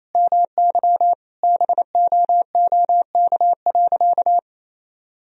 Morse code, heard as a keyed tone, about 28 wpm, 700 Hz. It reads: MY BOOK.
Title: Sentences from Top 1000 Words - Review Part 1 - 28wpm